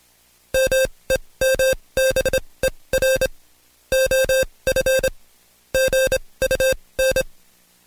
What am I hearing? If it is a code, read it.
Morse code, heard as a keyed tone, about 26 wpm, spaced as for 20 wpm, 530 Hz. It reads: MEMBER OF GUN
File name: Code Quick / AudioDownld2 / Seg136-Stepper9 19WPM